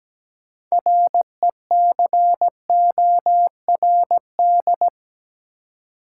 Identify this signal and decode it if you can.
Morse code, heard as a keyed tone, about 17 wpm, 700 Hz. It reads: RECORD
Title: Six Letter Words - 17wpm